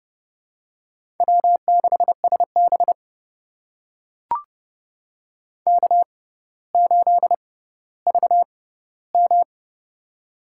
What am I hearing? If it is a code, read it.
Morse code, heard as a keyed tone, about 28 wpm, spaced as for 15 wpm, 700 Hz. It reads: W6SB K8VM
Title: USA Call Signs - 4 Characters - 6x Character Spacing - 30wpm